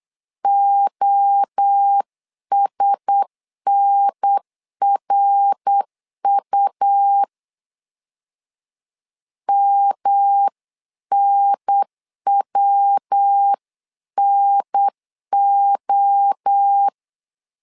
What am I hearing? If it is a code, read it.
Morse code, heard as a keyed tone, about 8 wpm, 800 Hz. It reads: OSNRU MNWNO
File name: LEZIONI / 4 Q1S 8wpm 2710